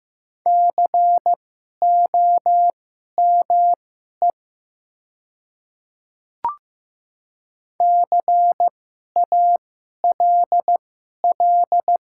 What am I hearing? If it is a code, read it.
Morse code, heard as a keyed tone, about 15 wpm, spaced as for 11 wpm, 700 Hz. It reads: COME CALL
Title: 2x Character Spacing - Top 100 Words - 15wpm